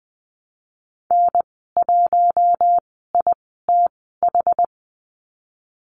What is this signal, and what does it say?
Morse code, heard as a keyed tone, about 20 wpm, spaced as for 15 wpm, 700 Hz. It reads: N1ITH
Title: 062 Lesson 11g - Letter h - Call Signs 20wpm